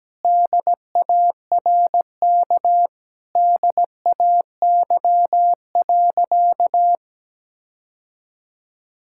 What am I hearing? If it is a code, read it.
Morse code, heard as a keyed tone, about 17 wpm, 700 Hz. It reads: DARK DAY.
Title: Sentences from Top 1000 Words - Review Part 5 - 17wpm